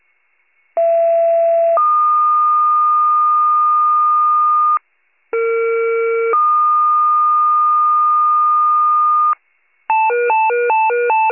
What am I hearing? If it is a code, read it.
Two-tone page, heard as a one-second tone, then a three-second tone, about 670 then 1220 Hz, about 460 then 1220 Hz. It is heard from 0.8 to 9.3 s.